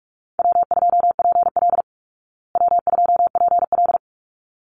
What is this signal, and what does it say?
Morse code, heard as a keyed tone, about 44 wpm, 700 Hz. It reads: W2PL W2PL
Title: Call Signs Repeated 3x - 45wpm